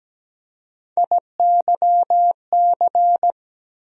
Morse code, heard as a keyed tone, about 17 wpm, 700 Hz. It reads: IYC
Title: USA Call Sign Postfixes - 3 Letters 17wpm